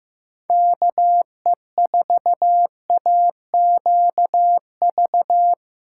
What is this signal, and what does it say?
Morse code, heard as a keyed tone, about 16 wpm, 700 Hz. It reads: KE4AQV
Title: USA Callsign DE Callsign 15wpm